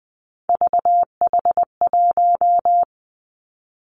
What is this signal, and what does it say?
Morse code, heard as a keyed tone, about 20 wpm, 700 Hz. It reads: VH1
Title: Three Challenging Letter-Number Combinations 20wpm